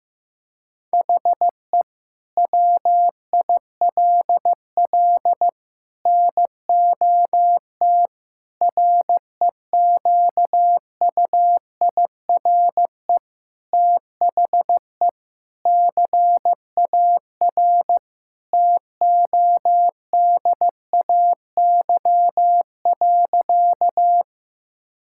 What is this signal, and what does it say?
Morse code, heard as a keyed tone, about 15 wpm, 700 Hz. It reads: HE WILL NOT REQUIRE THE CAR TODAY.
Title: Sentences from Top 1000 Words - Review Part 6 - 15wpm